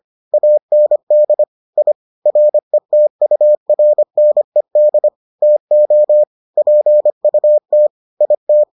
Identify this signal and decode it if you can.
Morse code, heard as a keyed tone, about 25 wpm, 600 Hz. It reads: AND I RETURNED TO PUT IT